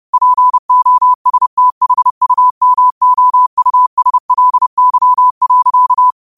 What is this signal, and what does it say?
Morse code, heard as a keyed tone, about 30 wpm, 1000 Hz. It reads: POSTHUMOUSLY.